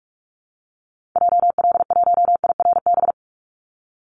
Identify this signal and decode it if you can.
Morse code, heard as a keyed tone, about 44 wpm, 700 Hz. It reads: JL1IRB
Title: Call Signs Repeated 3x - 45wpm